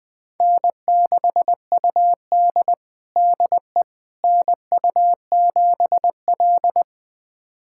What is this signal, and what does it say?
Morse code, heard as a keyed tone, about 20 wpm, 700 Hz. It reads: N6UD DE NU7L
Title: USA Callsign DE Callsign - 4 Characters - 20wpm